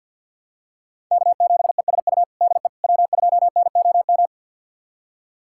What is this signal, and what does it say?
Morse code, heard as a keyed tone, about 53 wpm, 690 Hz. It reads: K7ESU DE W2NOM
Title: USA Callsign DE Callsign - 5 Characters - 50wpm